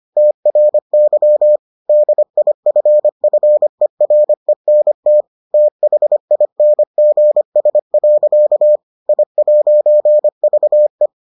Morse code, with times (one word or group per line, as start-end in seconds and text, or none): 0.2-1.6 TRY
1.9-5.2 DIFFERENT
5.5-8.8 THINGS.
9.1-11.1 I'VE